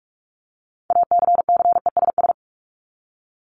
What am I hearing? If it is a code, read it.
Morse code, heard as a keyed tone, about 44 wpm, 700 Hz. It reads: ACCESS
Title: Top 501-750 Words 45wpm